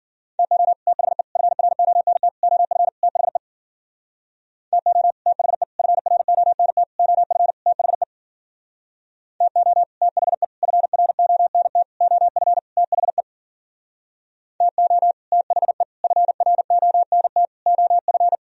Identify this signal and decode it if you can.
Morse code, heard as a keyed tone, about 52 wpm, 700 Hz. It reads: TO THE FRONT OF THE TO THE FRONT OF THE TO THE FRONT OF THE TO THE FRONT OF